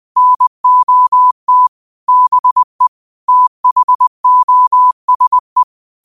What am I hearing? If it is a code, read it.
Morse code, heard as a keyed tone, about 20 wpm, 1000 Hz. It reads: NOT BE THOSE
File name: qotdfiles / 20wpm